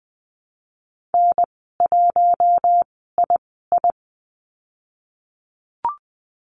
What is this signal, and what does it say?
Morse code, heard as a keyed tone, about 20 wpm, spaced as for 15 wpm, 700 Hz. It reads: N1II
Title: 055 Lesson 10g - Letter r - Call Signs 20wpm